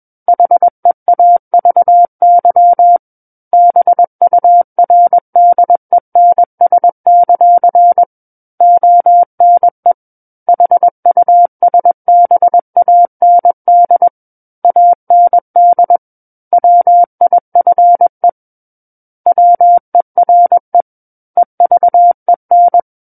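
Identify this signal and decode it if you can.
Morse code, heard as a keyed tone, about 21 wpm, 700 Hz. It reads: HEAVY BURDENS; ONE HUSBAND AND WIFE WERE EVEN
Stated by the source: Morse code practice reading of a book chapter at 21 words per minute.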